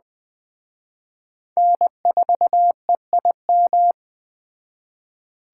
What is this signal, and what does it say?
Morse code, heard as a keyed tone, about 20 wpm, 700 Hz. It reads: N4EIM